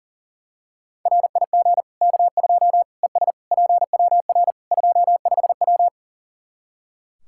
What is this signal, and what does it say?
Morse code, heard as a keyed tone, about 40 wpm, 700 Hz. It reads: RIG K2 ES PWR 25W
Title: Practice QSO - Exchange 2 - Info - 40wpm